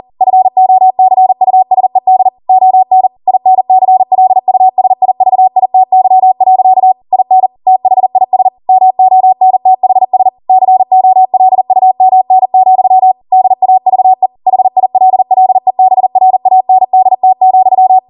Morse code, heard as a keyed tone, about 40 wpm, 750 Hz. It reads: FOCUSED ON INCLUSIVITY. IN THIS MONTHS COLUMN, DAVE HILLEBRANDT,